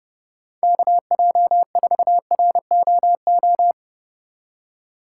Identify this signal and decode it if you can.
Morse code, heard as a keyed tone, about 30 wpm, 700 Hz. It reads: KJ4ROO
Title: USA Call Signs - 6 Characters - 3x Character Spacing - 30wpm